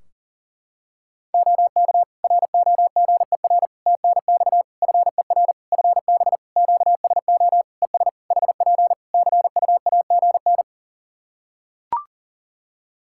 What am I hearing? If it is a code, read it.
Morse code, heard as a keyed tone, about 40 wpm, 700 Hz. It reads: OK ROGER TNX FER FB QSO ES HP CUAGN
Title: Practice QSO - Exchange 4 - Ending - 40wpm